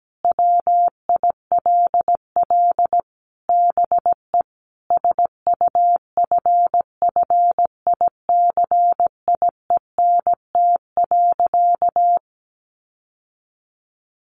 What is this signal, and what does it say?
Morse code, heard as a keyed tone, about 17 wpm, 700 Hz. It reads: WILL BE SUFFICIENT.
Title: Sentences from Top 1400 Words 17wpm